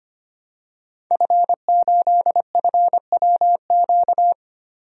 Morse code, heard as a keyed tone, about 25 wpm, 700 Hz. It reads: F8FWQ
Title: DXCC-100 Call Signs - 5 Characters - Repeated 3x - 25wpm